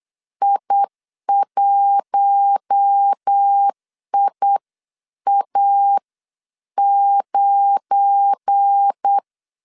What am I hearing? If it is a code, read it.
Morse code, heard as a keyed tone, about 8 wpm, spaced as for 7 wpm, 800 Hz. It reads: I1IA9
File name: CW / DET / 1 9 1500C S B 8wpm 5624